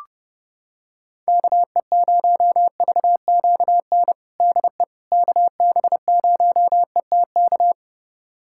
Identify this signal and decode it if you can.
Morse code, heard as a keyed tone, about 30 wpm, 700 Hz. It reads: KE0VQN DE KB0ETK
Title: USA Callsign DE Callsign 30wpm